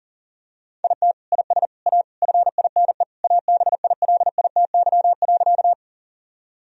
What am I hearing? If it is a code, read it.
Morse code, heard as a keyed tone, about 40 wpm, 700 Hz. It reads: IT IS A FINE ABILITY.
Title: Sentences from Top 1000 Words - Review Part 3 - 40wpm